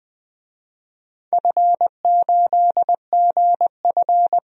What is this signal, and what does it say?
Morse code, heard as a keyed tone, about 20 wpm, 700 Hz. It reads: F8GF
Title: DXCC-100 Call Signs - 4 Characters - 20wpm